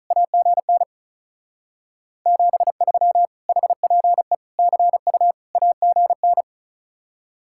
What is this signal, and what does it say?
Morse code, heard as a keyed tone, about 44 wpm, 700 Hz. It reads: AGN 73 HPE CU AGN
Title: Speed-Racing Common QSO Phrases - 35wpm